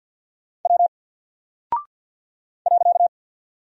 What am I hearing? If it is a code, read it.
Morse code, heard as a keyed tone, about 48 wpm, 700 Hz. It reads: W .
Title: Single Character (Letter:Number:Punctuation:Prosign) Alt 50wpm